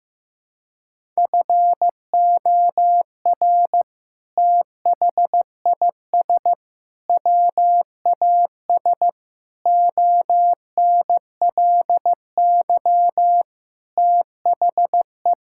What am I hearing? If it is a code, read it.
Morse code, heard as a keyed tone, about 16 wpm, 700 Hz. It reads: FOR THIS WAS ONLY THE
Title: Sentences from Top 1000 Words - Review Part 5 - 15wpm